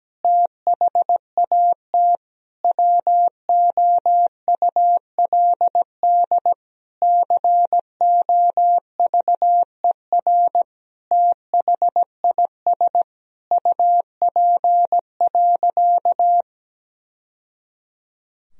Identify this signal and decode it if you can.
Morse code, heard as a keyed tone, about 17 wpm, 700 Hz. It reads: THAT WOULD COVER THIS UP.
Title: Sentences from Top 1000 Words - Review Part 1 - 17wpm